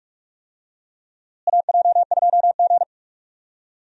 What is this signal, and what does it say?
Morse code, heard as a keyed tone, about 46 wpm, 700 Hz. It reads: AJ2G
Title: USA Call Signs - 4 Characters - Repeated 2x - 45wpm